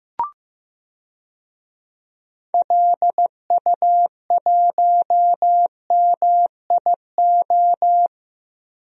Morse code, heard as a keyed tone, about 15 wpm, 700 Hz. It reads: LU1MIO
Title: DXCC-100 Call Signs - 6 Characters - 15wpm